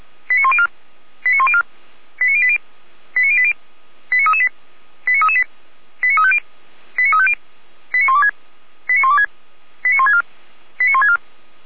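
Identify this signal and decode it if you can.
Five-tone selective call (ZVEI), heard as quick runs of five tones, tones of about 70 ms. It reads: 89104 89104 89080 89080 89338 89338 89350 89350 89126 89126 89164 89164
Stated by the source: ZVEI five-tone call received by SDR.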